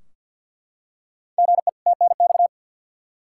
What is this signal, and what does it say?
Morse code, heard as a keyed tone, about 49 wpm, 700 Hz. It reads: GE TNX
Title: Practice QSO - Exchange 1 - Beginning - 50wpm